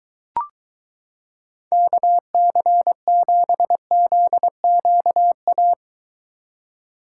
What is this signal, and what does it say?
Morse code, heard as a keyed tone, about 23 wpm, 700 Hz. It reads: KC7ZQA